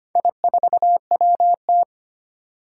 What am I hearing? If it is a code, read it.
Morse code, heard as a keyed tone, about 25 wpm, 700 Hz. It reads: I4WT
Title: Four Challenging Letter-Number Combinations 25wpm